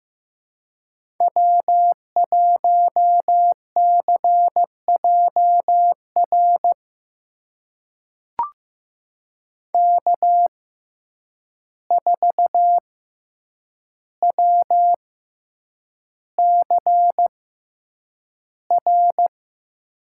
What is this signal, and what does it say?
Morse code, heard as a keyed tone, about 15 wpm, spaced as for 8 wpm, 700 Hz. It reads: W1CJR K4WCR
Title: USA Call Signs - 5 Characters - 6x Character Spacing - 15wpm